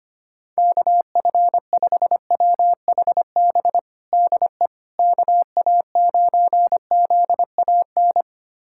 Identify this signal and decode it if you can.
Morse code, heard as a keyed tone, about 25 wpm, 700 Hz. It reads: KF5WHB DE KA9ZAN
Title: USA Callsign DE Callsign - 6 Characters - 25wpm